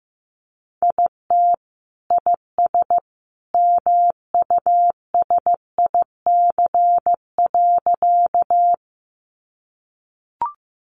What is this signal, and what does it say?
Morse code, heard as a keyed tone, about 14 wpm, 700 Hz. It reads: IT IS MUSIC.
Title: Sentences from Top 300 Words 15wpm